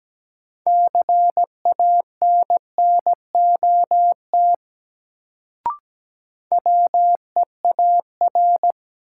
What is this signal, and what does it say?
Morse code, heard as a keyed tone, about 17 wpm, 700 Hz. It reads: CANNOT WEAR
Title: Sentences from Top 1000 Words - Review Part 6 - New Word Drill - 17wpm